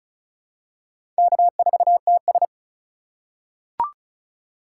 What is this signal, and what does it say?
Morse code, heard as a keyed tone, about 36 wpm, 700 Hz. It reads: K4TS
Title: USA Call Signs - 2x Character Spacing - 35wpm